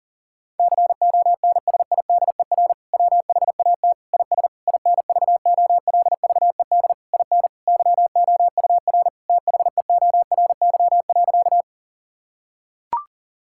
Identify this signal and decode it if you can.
Morse code, heard as a keyed tone, about 40 wpm, 700 Hz. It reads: CONSIDER WHAT IS INVOLVED IN YOUR THEORY.
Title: Sentences from Top 1000 Words - Review Part 5 - 40wpm